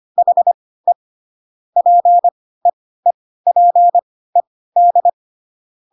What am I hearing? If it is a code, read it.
Morse code, heard as a keyed tone, about 25 wpm, spaced as for 16 wpm, 700 Hz. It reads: HE PEEPED
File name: Pinocchio / Pinocchio ch1 10 25 ik1hge